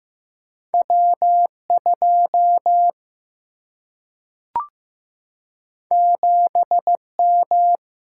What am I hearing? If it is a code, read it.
Morse code, heard as a keyed tone, about 15 wpm, 700 Hz. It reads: W 2 7 M
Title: Single-Letter Single-Number 15wpm